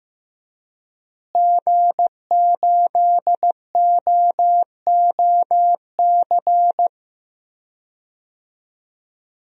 Morse code, heard as a keyed tone, about 15 wpm, 700 Hz. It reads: G8OOC
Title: DXCC-100 Call Signs - 5 Characters - Repeated 3x - 15wpm